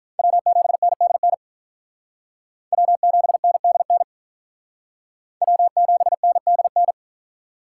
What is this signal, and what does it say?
Morse code, heard as a keyed tone, about 46 wpm, 700 Hz. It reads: W7NDN W7NDN W7NDN